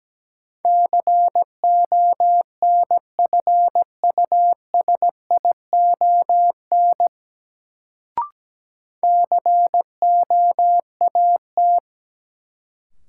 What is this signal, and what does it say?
Morse code, heard as a keyed tone, about 17 wpm, 700 Hz. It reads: CONFUSION COAT
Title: Sentences from Top 1600 Words - New Word Drill - 17wpm